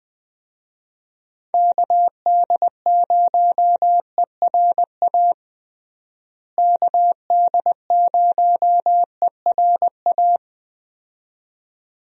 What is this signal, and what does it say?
Morse code, heard as a keyed tone, about 20 wpm, 700 Hz. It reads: KD0ERA KD0ERA
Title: USA Call Signs - 6 Characters - Repeated 2x - 20wpm